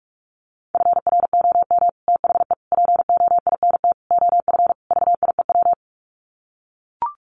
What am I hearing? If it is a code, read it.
Morse code, heard as a keyed tone, about 44 wpm, 700 Hz. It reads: FROM THE POINT OF VIEW
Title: Sets of 5 Words 45wpm